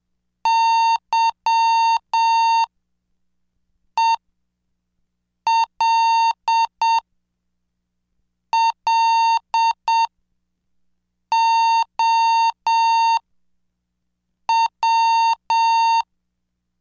Morse code, heard as a keyed tone, about 7 wpm, spaced as for 4 wpm, 910 Hz. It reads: YELLOW
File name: cursos / CDestudo / 20 Track 20